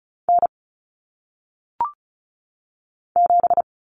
Morse code, heard as a keyed tone, about 36 wpm, 700 Hz. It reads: N 7
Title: Common Cut Numbers - 1-Digit 35wpm